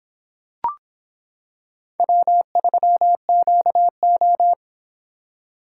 Morse code, heard as a keyed tone, about 25 wpm, 700 Hz. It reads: W3QO